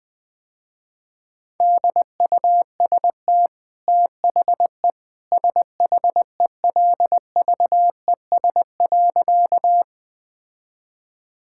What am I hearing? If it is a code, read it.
Morse code, heard as a keyed tone, about 20 wpm, 700 Hz. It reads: DUST THE SHELVES.
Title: Easy Sentences with Hints - 20wpm